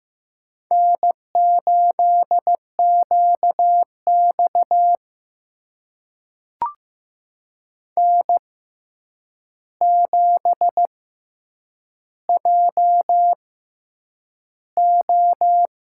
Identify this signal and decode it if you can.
Morse code, heard as a keyed tone, about 15 wpm, spaced as for 8 wpm, 700 Hz. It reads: N8QX N7JO
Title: USA Call Signs - 4 Characters - 6x Character Spacing - 15wpm